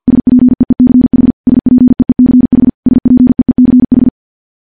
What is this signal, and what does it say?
Morse code, heard as a keyed tone, about 51 wpm, 250 Hz. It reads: SPEECH SPEECH SPEECH
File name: morse / morse-training-50WPM